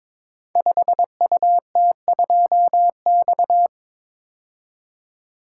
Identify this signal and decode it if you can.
Morse code, heard as a keyed tone, about 22 wpm, 700 Hz. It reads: 5UT2X
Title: Code Groups - 5 Letter:Numbers - 22wpm